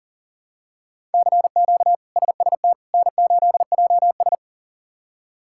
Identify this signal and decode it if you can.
Morse code, heard as a keyed tone, about 40 wpm, 700 Hz. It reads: CQ SST N8JS
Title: K1USN SST QSO Practice - 40wpm